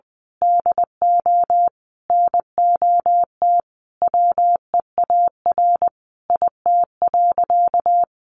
Morse code, heard as a keyed tone, about 20 wpm, 700 Hz. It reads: DO NOT WEAR IT.
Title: Sentences from Top 1500 Words 20wpm